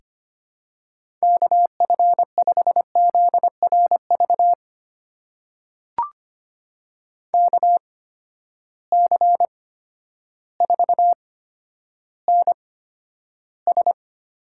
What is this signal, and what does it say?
Morse code, heard as a keyed tone, about 25 wpm, spaced as for 11 wpm, 700 Hz. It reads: KF5ZRV KC4NS
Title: USA Call Signs - 6 Characters - 8x Character Spacing - 25wpm